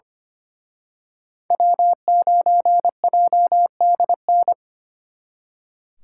Morse code, heard as a keyed tone, about 31 wpm, 700 Hz. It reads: W9JDN
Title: Speed Racing - USA 1x3 Call Signs 25wpm